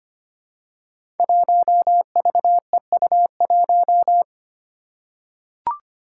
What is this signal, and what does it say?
Morse code, heard as a keyed tone, about 25 wpm, 700 Hz. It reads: 1VEU1
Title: Five Challenging Letter-Number Combinations 25wpm